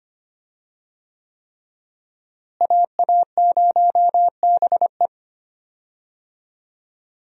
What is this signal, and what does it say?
Morse code, heard as a keyed tone, about 25 wpm, 700 Hz. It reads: AA0BE
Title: USA Call Signs - 5 Characters - Repeated 3x - 25wpm